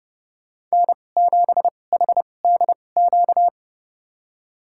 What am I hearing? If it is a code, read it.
Morse code, heard as a keyed tone, about 30 wpm, spaced as for 22 wpm, 700 Hz. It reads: N7HDQ
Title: USA Call Signs - 5 Characters - 2x Character Spacing - 30wpm